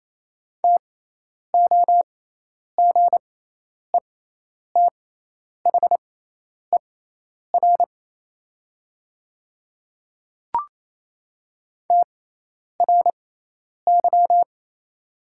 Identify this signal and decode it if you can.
Morse code, heard as a keyed tone, about 27 wpm, spaced as for 10 wpm, 700 Hz. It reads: TOGETHER TRY
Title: 6x Character Spacing - Top 300 Words - 28wpm